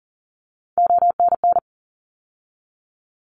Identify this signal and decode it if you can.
Morse code, heard as a keyed tone, about 39 wpm, 700 Hz. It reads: ONN
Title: ARRL Sections 40wpm